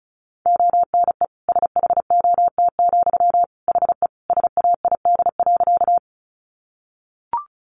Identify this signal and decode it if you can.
Morse code, heard as a keyed tone, about 34 wpm, 700 Hz. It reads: ONE SHOT, HE SAID.
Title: Sentences from Top 1100 Words 35wpm